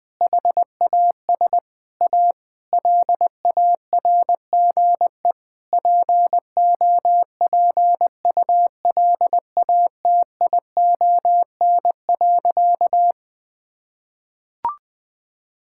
Morse code, heard as a keyed tone, about 20 wpm, 700 Hz. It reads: HAS A LARGE POPULATION.
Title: Sentences with 4-Word Phrases - 20wpm